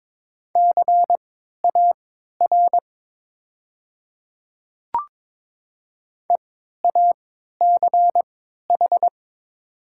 Morse code, heard as a keyed tone, about 22 wpm, spaced as for 13 wpm, 700 Hz. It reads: CAR EACH
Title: 3x Character Spacing - Top 300 Words - 22wpm